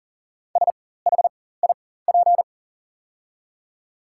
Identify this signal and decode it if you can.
Morse code, heard as a keyed tone, about 37 wpm, spaced as for 18 wpm, 690 Hz. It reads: SHIP
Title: 4x Character Spacing - Top 500 Words - 40wpm